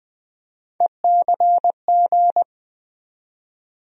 Morse code, heard as a keyed tone, about 20 wpm, 700 Hz. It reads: ECG
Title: USA Call Sign Postfixes - 3 Letters 20wpm